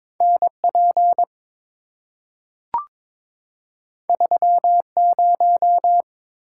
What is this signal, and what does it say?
Morse code, heard as a keyed tone, about 22 wpm, 700 Hz. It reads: N P 3 0
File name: practice / 22wpm / Two Letter-Number Combinations - 22wpm